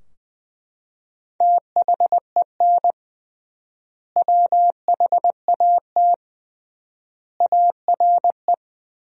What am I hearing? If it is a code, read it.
Morse code, heard as a keyed tone, about 21 wpm, 700 Hz. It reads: THEN WHAT ARE
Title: 3x Word-Spacing - Sentences from Top 100 Words - 20wpm